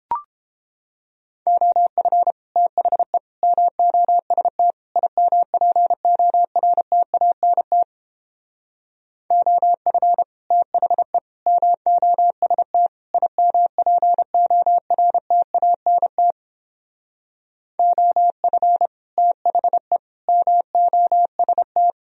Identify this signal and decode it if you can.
Morse code, heard as a keyed tone, about 29 wpm, 700 Hz. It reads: OF THE MOST IMPORTANT OF THE MOST IMPORTANT OF THE MOST